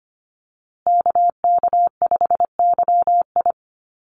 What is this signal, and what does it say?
Morse code, heard as a keyed tone, about 25 wpm, 700 Hz. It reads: KK5YI
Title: USA Call Signs Repeated 2x 25wpm